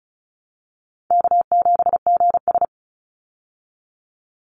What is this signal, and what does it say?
Morse code, heard as a keyed tone, about 34 wpm, 700 Hz. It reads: K7GS
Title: Easy Call Signs Repeated 3x - 35wpm